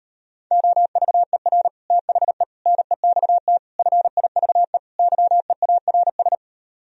Morse code, heard as a keyed tone, about 38 wpm, 700 Hz. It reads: OVER THE NEXT FIVE YEARS